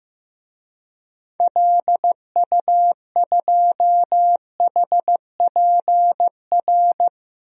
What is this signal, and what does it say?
Morse code, heard as a keyed tone, about 15 wpm, 700 Hz. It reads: LU2HPR
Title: DXCC-100 Call Signs - 6 Characters - Repeated 2x - 15wpm